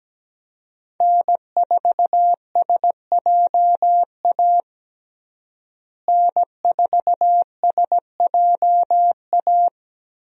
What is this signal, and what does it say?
Morse code, heard as a keyed tone, about 17 wpm, 700 Hz. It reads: N4SJA N4SJA
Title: USA Call Signs - 5 Characters - Repeated 2x - 17wpm